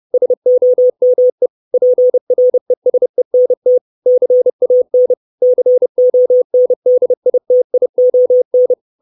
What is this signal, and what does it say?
Morse code, heard as a keyed tone, about 30 wpm, 500 Hz. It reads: SOME PRESENT CAN CONDITION
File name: 632 Words 30WPM